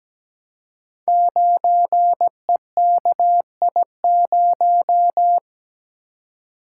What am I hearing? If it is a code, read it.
Morse code, heard as a keyed tone, about 17 wpm, 700 Hz. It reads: 9EKI0
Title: Code Groups - 5 Letter:Numbers - 17wpm